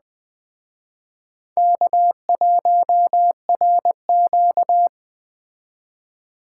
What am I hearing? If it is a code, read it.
Morse code, heard as a keyed tone, about 20 wpm, 700 Hz. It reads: K1RQ